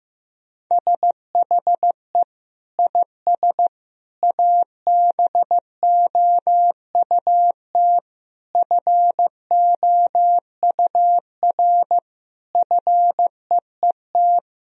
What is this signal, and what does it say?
Morse code, heard as a keyed tone, about 15 wpm, 700 Hz. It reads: SHE IS ABOUT FOUR FEET
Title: Sentences from Top 1000 Words - Review Part 1 - 15wpm